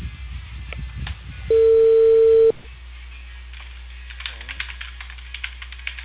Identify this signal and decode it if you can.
Ringback tone (China, 450 Hz).